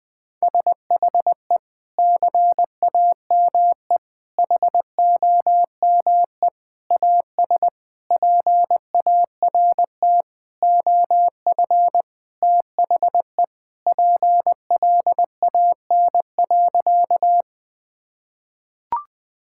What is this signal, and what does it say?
Morse code, heard as a keyed tone, about 21 wpm, 700 Hz. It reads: SHE CAME HOME AS PART OF THE PLAN.
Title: Sentences with 4-Word Phrases - 20wpm